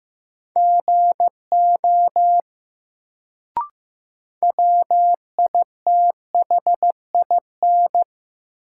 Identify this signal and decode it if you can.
Morse code, heard as a keyed tone, about 15 wpm, 700 Hz. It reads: GO WITHIN
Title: Sets of 2 Words - New Word Drill - 15wpm